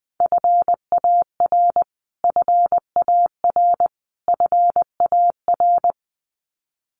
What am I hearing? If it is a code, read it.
Morse code, heard as a keyed tone, about 20 wpm, 700 Hz. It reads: FAR FAR FAR
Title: Top 500 Words Repeated 20wpm